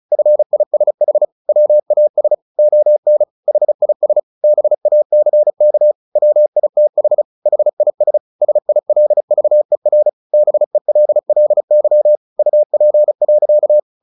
Morse code, heard as a keyed tone, about 34 wpm, 600 Hz. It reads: FISH WAS ON HIS BACK WITH HIS SILVER BELLY UP.
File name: the old man and the sea 2 35wpm